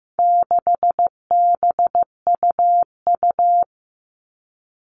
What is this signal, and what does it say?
Morse code, heard as a keyed tone, about 15 wpm, 700 Hz. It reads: 6BUU
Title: All Cut Numbers - 4-Digit 15wpm